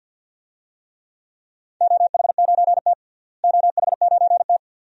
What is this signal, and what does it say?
Morse code, heard as a keyed tone, about 49 wpm, 700 Hz. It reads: OH9T OH9T
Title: DXCC-100 Call Signs - 4 Characters - Repeated 2x - 50wpm